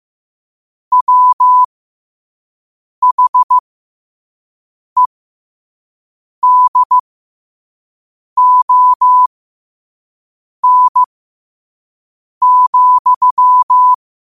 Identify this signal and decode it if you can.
Morse code, heard as a keyed tone, about 15 wpm, spaced as for 5 wpm, 1000 Hz. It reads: WHEDON,
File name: qotdfiles / 5wpm